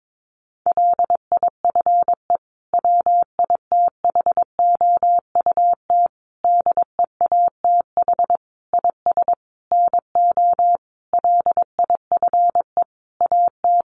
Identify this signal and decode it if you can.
Morse code, heard as a keyed tone, about 22 wpm, 700 Hz. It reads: LIFE WITHOUT DEATH IS NO LIFE AT
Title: Sentences from Top 1100 Words 22wpm